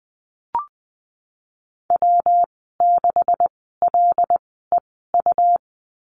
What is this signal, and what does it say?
Morse code, heard as a keyed tone, about 20 wpm, spaced as for 15 wpm, 700 Hz. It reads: W6LEU
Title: 245 Lesson 39g - Letter z - Call Signs 20wpm